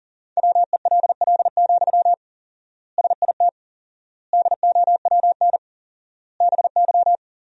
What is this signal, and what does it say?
Morse code, heard as a keyed tone, about 37 wpm, 700 Hz. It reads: WELL, SIT DOWN BY
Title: 4x Word-Spacing - Sentences from Top 500 Words - 40wpm